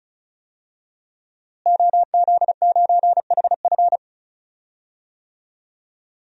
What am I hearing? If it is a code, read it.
Morse code, heard as a keyed tone, about 34 wpm, 700 Hz. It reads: OZ9HF
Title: DXCC-100 Call Signs - 5 Characters - Repeated 3x - 35wpm